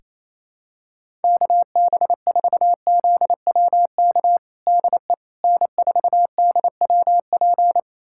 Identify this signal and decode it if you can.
Morse code, heard as a keyed tone, about 28 wpm, 700 Hz. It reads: KB4ZWK DE N4DWP
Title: USA Callsign DE Callsign 28wpm